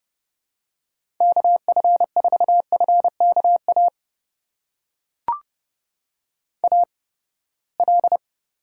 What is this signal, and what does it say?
Morse code, heard as a keyed tone, about 32 wpm, spaced as for 21 wpm, 700 Hz. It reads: KF4FKA AL
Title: USA Call Signs - 8x Character Spacing - 30wpm